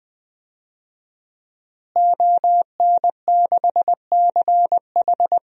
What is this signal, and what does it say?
Morse code, heard as a keyed tone, about 20 wpm, 700 Hz. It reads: ON6CH
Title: DXCC-100 Call Signs - 5 Characters - Repeated 3x - 20wpm